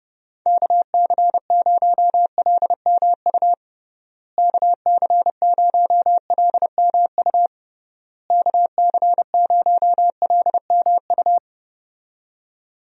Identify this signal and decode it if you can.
Morse code, heard as a keyed tone, about 30 wpm, 700 Hz. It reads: KC0LMU KC0LMU KC0LMU
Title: USA Call Signs - 6 Characters - Repeated 3x - 30wpm